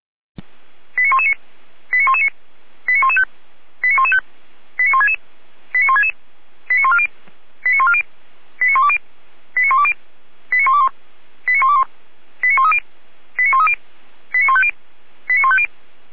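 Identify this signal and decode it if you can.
Five-tone selective call (ZVEI), heard as quick runs of five tones, tones of about 80 ms. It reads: 89119 89119 89115 89115 89166 89166 89140 89140 89120 89120 89121 89121 89130 89130 89150 89150
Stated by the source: A ZVEI five-tone selective call received by SDR.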